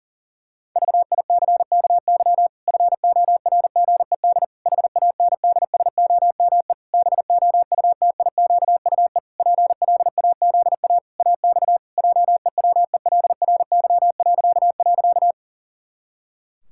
Morse code, heard as a keyed tone, about 40 wpm, 700 Hz. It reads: VICKY FORGED HANDSOME BOUTIQUE PLAZA AX JEWELRY..
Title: Pangram Challenge v2 - 40wpm